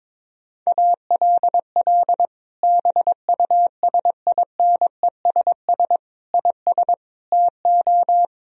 Morse code, heard as a keyed tone, about 22 wpm, 700 Hz. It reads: ALL BUSINESS IS TO
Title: Sentences from Top 1000 Words - Review Part 3 - 22wpm